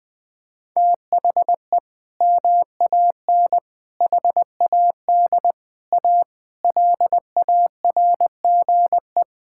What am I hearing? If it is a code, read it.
Morse code, heard as a keyed tone, about 20 wpm, 700 Hz. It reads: THE MAN HAD A LARGE
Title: Sentences from Top 1000 Words - Review Part 6 - 20wpm